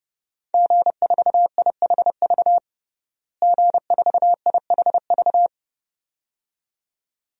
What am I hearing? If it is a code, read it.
Morse code, heard as a keyed tone, about 32 wpm, 700 Hz. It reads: G4IHV G4IHV
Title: DXCC-100 Call Signs - 5 Characters - Repeated 2x - 30wpm